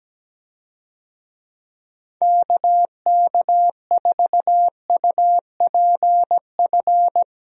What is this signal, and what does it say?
Morse code, heard as a keyed tone, about 17 wpm, 700 Hz. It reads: KK4UPF
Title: USA Call Signs - 6 Characters - 17wpm